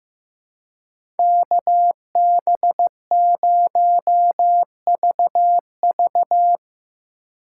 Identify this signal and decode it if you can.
Morse code, heard as a keyed tone, about 15 wpm, 700 Hz. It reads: KB0VV
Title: USA Call Signs - 5 Characters - 4x Character Spacing - 15wpm